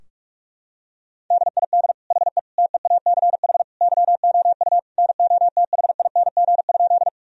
Morse code, heard as a keyed tone, about 46 wpm, 700 Hz. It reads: DID HE TEACH YOU NOTHING?
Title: Sentences from Top 1000 Words - Review Part 2 - 45wpm